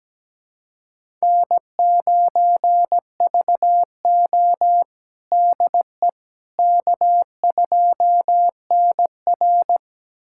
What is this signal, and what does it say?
Morse code, heard as a keyed tone, about 17 wpm, 700 Hz. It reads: N9VO DE K2NR
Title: USA Callsign DE Callsign - 4 Characters - 17wpm